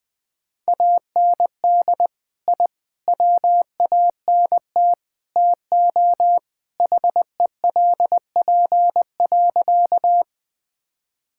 Morse code, heard as a keyed tone, about 20 wpm, 700 Hz. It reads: AND I WANT TO HELP.
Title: Sentences with 4-Word Phrases - Spoken Beginning - 20wpm